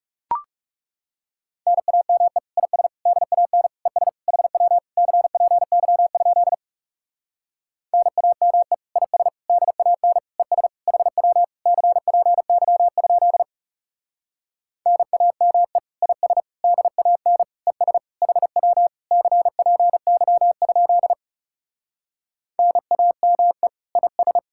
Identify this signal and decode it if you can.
Morse code, heard as a keyed tone, about 40 wpm, 700 Hz. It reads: NAME IS DAN ES HW CPY? NAME IS DAN ES HW CPY? NAME IS DAN ES HW CPY? NAME IS